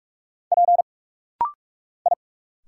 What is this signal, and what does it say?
Morse code, heard as a keyed tone, about 46 wpm, 700 Hz. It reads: P I
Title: Single Letters - Rapid-Fire 45wpm